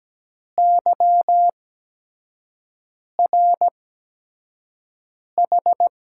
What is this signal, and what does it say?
Morse code, heard as a keyed tone, about 17 wpm, spaced as for 5 wpm, 700 Hz. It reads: YRH
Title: USA Call Signs - 6 Characters - 8x Character Spacing - 17wpm